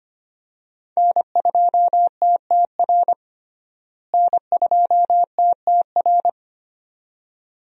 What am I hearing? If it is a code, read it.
Morse code, heard as a keyed tone, about 25 wpm, 700 Hz. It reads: N2TTR N2TTR
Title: USA Call Signs - 5 Characters - Repeated 2x - 25wpm